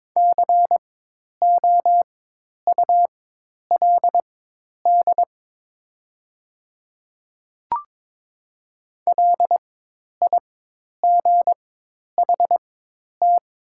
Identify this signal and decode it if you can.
Morse code, heard as a keyed tone, about 22 wpm, spaced as for 10 wpm, 700 Hz. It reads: COULD LIGHT
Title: 4x Character Spacing - Top 200 Words - 22wpm